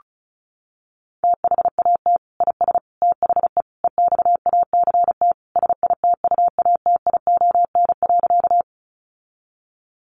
Morse code, heard as a keyed tone, about 34 wpm, 700 Hz. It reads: THAT IS THE EXACT SITUATION.
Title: Sentences from Top 1300 Words 35wpm